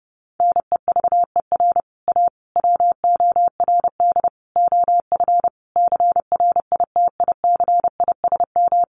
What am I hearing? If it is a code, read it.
Morse code, heard as a keyed tone, about 30 wpm, 700 Hz. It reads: NEVER A WORD OF CRITICISM
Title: Sentences from Top 1300 Words 30wpm